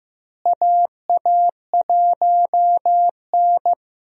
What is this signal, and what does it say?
Morse code, heard as a keyed tone, about 15 wpm, 700 Hz. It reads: AA1N
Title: USA Call Signs - 4 Characters - Repeated 3x - 15wpm